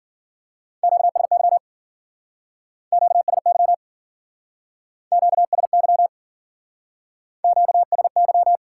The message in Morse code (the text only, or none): QSY QSY QSY QSY